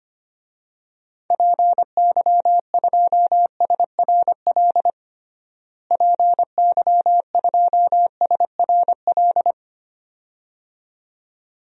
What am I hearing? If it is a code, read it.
Morse code, heard as a keyed tone, about 25 wpm, 700 Hz. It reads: PY2SRL PY2SRL
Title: DXCC-100 Call Signs - 6 Characters - Repeated 2x - 25wpm